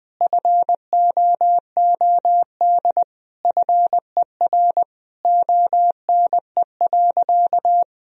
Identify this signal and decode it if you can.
Morse code, heard as a keyed tone, about 20 wpm, 700 Hz. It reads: FOOD FER ONE.
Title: Sentences from Top 300 Words and Abbreviations 20wpm